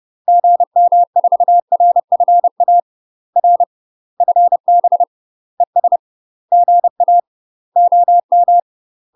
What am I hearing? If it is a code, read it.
Morse code, heard as a keyed tone, about 30 wpm, 700 Hz. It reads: GM4RFA R FB ES GA OM
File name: UK MASTER QSOs TRAINING / 30WPM / UK MASTER QSOs 0040